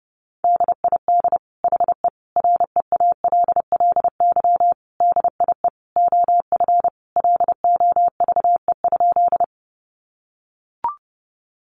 Morse code, heard as a keyed tone, about 30 wpm, 700 Hz. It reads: DID HE REALLY DIE OF LOVE?
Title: Sentences from Top 1300 Words 30wpm